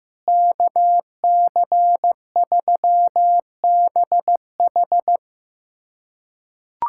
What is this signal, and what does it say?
Morse code, heard as a keyed tone, about 15 wpm, 700 Hz. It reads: KC3BH
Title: USA Callsign DE Callsign - 5 Characters - 15wpm